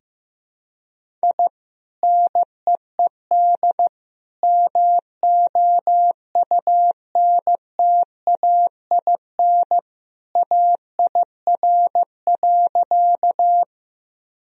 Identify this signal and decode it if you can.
Morse code, heard as a keyed tone, about 15 wpm, 700 Hz. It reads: I NEED MOUNTAIN AIR.
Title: Sentences from Top 1000 Words - Review Part 1 - 15wpm